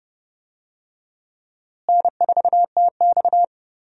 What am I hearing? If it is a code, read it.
Morse code, heard as a keyed tone, about 30 wpm, 700 Hz. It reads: N4TX
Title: USA Call Signs - 4 Characters - Repeated 3x - 30wpm